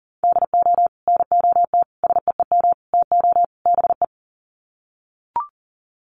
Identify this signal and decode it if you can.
Morse code, heard as a keyed tone, about 40 wpm, 700 Hz. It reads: DO NOT SEEM TO BE
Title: Sets of 5 Words Encore2 40wpm